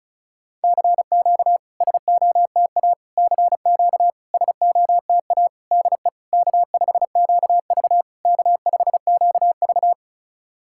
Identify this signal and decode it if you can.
Morse code, heard as a keyed tone, about 37 wpm, 700 Hz. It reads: CQ SOTA CQ SOTA DE K5QV K5QV
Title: SOTA QSO Practice 35wpm